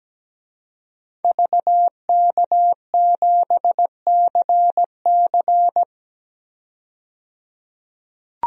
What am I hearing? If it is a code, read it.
Morse code, heard as a keyed tone, about 17 wpm, 700 Hz. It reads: VK7CC
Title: DXCC-100 Call Signs - 5 Characters - 17wpm